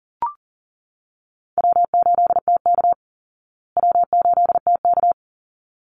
Morse code, heard as a keyed tone, about 40 wpm, 700 Hz. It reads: W8TK W8TK
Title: Easy Call Signs Repeated 2x - 40wpm